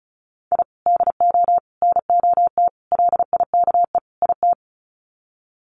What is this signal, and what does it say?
Morse code, heard as a keyed tone, about 34 wpm, 700 Hz. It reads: I DO NOT LIKE IT
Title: Sets of 5 Words Encore 35wpm